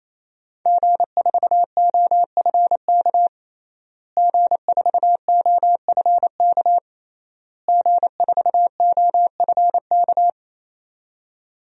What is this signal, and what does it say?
Morse code, heard as a keyed tone, about 28 wpm, 700 Hz. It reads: G4OFK G4OFK G4OFK
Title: DXCC-100 Call Signs - 5 Characters - Repeated 3x - 28wpm